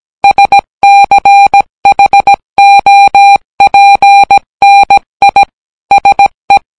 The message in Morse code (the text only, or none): SCHOPNI SE